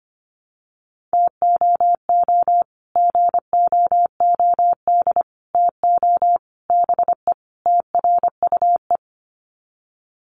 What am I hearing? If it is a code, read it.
Morse code, heard as a keyed tone, about 25 wpm, 700 Hz. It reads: TOO GOOD TO BE TRUE
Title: Sets of 5 Words Encore 25wpm